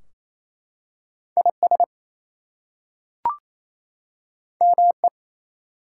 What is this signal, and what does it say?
Morse code, heard as a keyed tone, about 27 wpm, 700 Hz. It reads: IS ME
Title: Two Letter Words - 28wpm